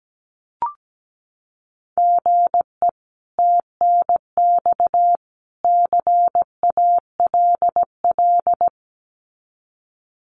Morse code, heard as a keyed tone, about 17 wpm, 700 Hz. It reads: GE TNX CALL
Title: Common QSO Phrases 17wpm